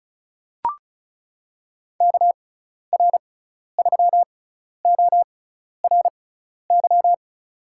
Morse code, heard as a keyed tone, about 34 wpm, spaced as for 12 wpm, 700 Hz. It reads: KR3ORY
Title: USA Call Signs - 6 Characters - 6x Character Spacing - 35wpm